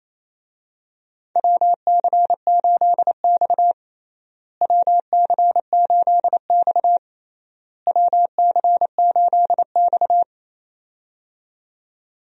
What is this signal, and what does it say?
Morse code, heard as a keyed tone, about 28 wpm, 700 Hz. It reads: WC8X WC8X WC8X
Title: USA Call Signs - 4 Characters - Repeated 3x - 28wpm